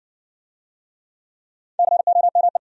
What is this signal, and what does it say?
Morse code, heard as a keyed tone, about 58 wpm, 700 Hz. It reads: COME